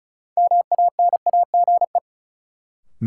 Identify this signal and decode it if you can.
Morse code, heard as a keyed tone, about 36 wpm, 700 Hz. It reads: MANAGE
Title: Sentences from Top 1400 Words - New Word Drill - 35wpm